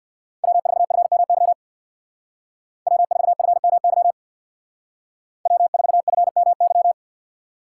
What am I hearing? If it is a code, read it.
Morse code, heard as a keyed tone, about 60 wpm, 700 Hz. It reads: W4FMY W4FMY W4FMY
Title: Speed Racing - USA 1x3 Call Signs 50wpm